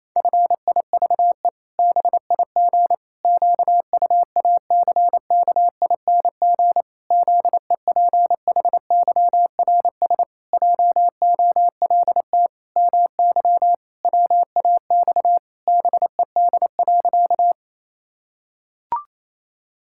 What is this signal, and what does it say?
Morse code, heard as a keyed tone, about 28 wpm, 700 Hz. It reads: FIVE BIG QUACKING ZEPHYRS JOLT MY WAX BED.
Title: Pangram Challenge v2 - 28wpm